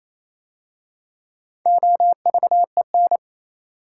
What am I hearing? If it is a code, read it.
Morse code, heard as a keyed tone, about 27 wpm, 700 Hz. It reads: OVEN
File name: Easy Words - Speed-Racing with Hints - 22wpm